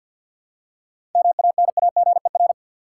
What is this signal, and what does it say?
Morse code, heard as a keyed tone, about 46 wpm, 700 Hz. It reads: MANAGER
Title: Seven Letter Words - 50wpm